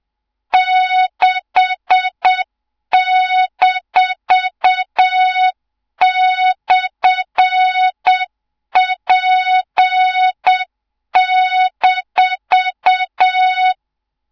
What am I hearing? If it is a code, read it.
Morse code, heard as a keyed tone, about 7 wpm, 3700 Hz, with noise. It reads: 6-/P-
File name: Morze-HA2MN / hang / 24 GYAKORLAT